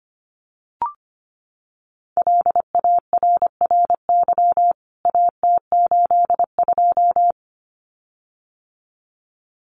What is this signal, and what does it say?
Morse code, heard as a keyed tone, about 25 wpm, 700 Hz. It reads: LARRY AT82
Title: CWT Exchange 25wpm